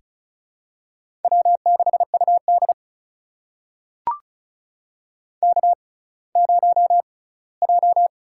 Morse code, heard as a keyed tone, about 36 wpm, spaced as for 20 wpm, 700 Hz. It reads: W6UD K0J
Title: USA Call Signs - 4 Characters - 6x Character Spacing - 35wpm